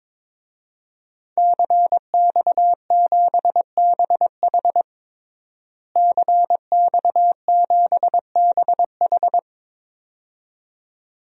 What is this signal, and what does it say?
Morse code, heard as a keyed tone, about 22 wpm, 700 Hz. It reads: CX7BH CX7BH
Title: DXCC-100 Call Signs - 5 Characters - Repeated 2x - 22wpm